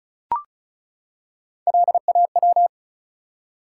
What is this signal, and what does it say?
Morse code, heard as a keyed tone, about 36 wpm, 700 Hz. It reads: LAW
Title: Three Letter Words - 35wpm